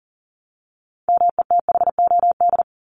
Morse code, heard as a keyed tone, about 40 wpm, 700 Hz. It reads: METHOD
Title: Top 501-750 Words 40wpm